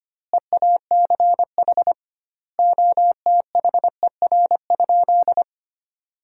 Morse code, heard as a keyed tone, about 25 wpm, 700 Hz. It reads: EACH OTHER?
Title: 2x Word-Spacing - Sentences from Top 200 Words - 25wpm